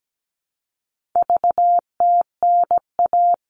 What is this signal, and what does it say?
Morse code, heard as a keyed tone, about 17 wpm, 700 Hz. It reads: VTNA
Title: All Cut Numbers - 4-Digit 17wpm